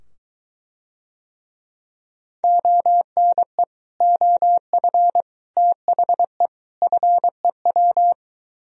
Morse code, heard as a keyed tone, about 23 wpm, 700 Hz. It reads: ONE OF THE FEW